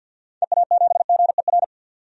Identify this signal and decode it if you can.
Morse code, heard as a keyed tone, about 51 wpm, 700 Hz. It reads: EA7GER
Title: DXCC-100 Call Signs - 6 Characters - 50wpm